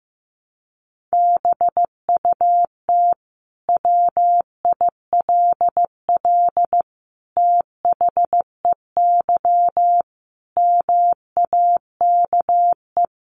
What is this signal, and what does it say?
Morse code, heard as a keyed tone, about 15 wpm, 700 Hz. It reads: BUT WILL THEY MAKE
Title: Sentences from Top 1000 Words 15wpm